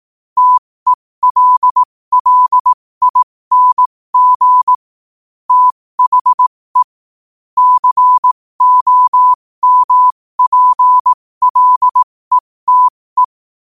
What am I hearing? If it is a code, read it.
Morse code, heard as a keyed tone, about 18 wpm, spaced as for 15 wpm, 1000 Hz. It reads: TELLING THE COMPLETE
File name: qotdfiles / 15wpm